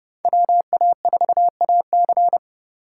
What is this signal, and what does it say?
Morse code, heard as a keyed tone, about 36 wpm, 700 Hz. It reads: WA4AC
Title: Speed Racing - USA 2x2 Call Signs 30wpm